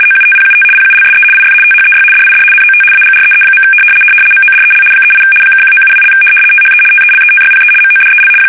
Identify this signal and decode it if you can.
RTTY, about 75 baud. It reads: OUTPUT PULSE IF THEY DID NOT AGREE. MODEL 38 (1972) THE TELETYPE MODEL
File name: fsk850-75B / teletype-corp